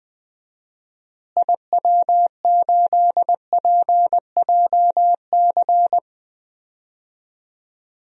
Morse code, heard as a keyed tone, about 20 wpm, 700 Hz. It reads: IW8PJC
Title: DXCC-100 Call Signs - 6 Characters - Repeated 2x - 20wpm